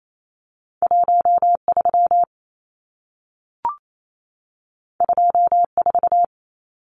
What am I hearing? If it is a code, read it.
Morse code, heard as a keyed tone, about 28 wpm, 700 Hz. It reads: 13 24
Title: Numbers - 2-Digit 28wpm